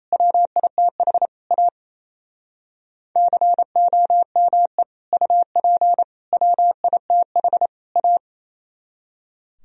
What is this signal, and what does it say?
Morse code, heard as a keyed tone, about 34 wpm, 700 Hz. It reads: WITH A COME UP WITH A
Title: Speed-Racing - Sets of 4 Words 28wpm